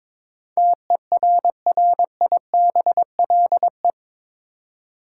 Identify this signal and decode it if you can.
Morse code, heard as a keyed tone, about 21 wpm, 700 Hz. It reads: TERRIBLE
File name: practice / 22wpm / Eight Letter Words - 22wpm